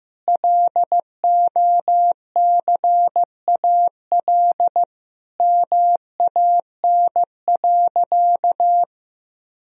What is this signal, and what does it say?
Morse code, heard as a keyed tone, about 15 wpm, 700 Hz. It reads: LOCAL MAN.
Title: Sentences from Top 1000 Words - Review Part 3 - 15wpm